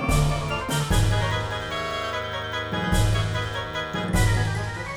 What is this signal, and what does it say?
There is no code.